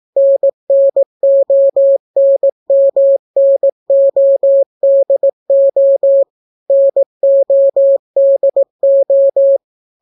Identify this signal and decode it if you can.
Morse code, heard as a keyed tone, about 18 wpm, 550 Hz. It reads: NNONMNODO NODO